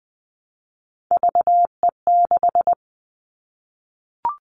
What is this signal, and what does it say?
Morse code, heard as a keyed tone, about 20 wpm, 700 Hz. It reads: VE6
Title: Callsign Prefixes for Top 100 DXCC - 20wpm